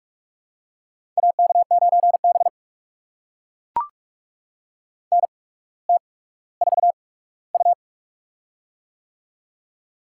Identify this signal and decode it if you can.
Morse code, heard as a keyed tone, about 44 wpm, spaced as for 19 wpm, 690 Hz. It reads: AK9B NT4U
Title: USA Call Signs - 4 Characters - 8x Character Spacing - 45wpm